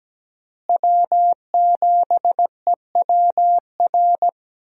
Morse code, heard as a keyed tone, about 17 wpm, 700 Hz. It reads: W7EWR
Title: USA Call Signs - 5 Characters - Repeated 2x - 17wpm